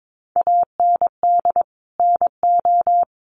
Morse code, heard as a keyed tone, about 22 wpm, 700 Hz. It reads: AND NO
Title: Sentences from Top 1100 Words 22wpm